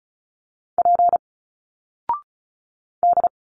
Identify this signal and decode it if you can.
Morse code, heard as a keyed tone, about 36 wpm, 700 Hz. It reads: P D
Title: Single Letters 35wpm